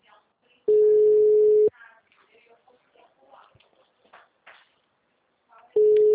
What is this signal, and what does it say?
Ringback tone (Europe, 425 Hz), with speech.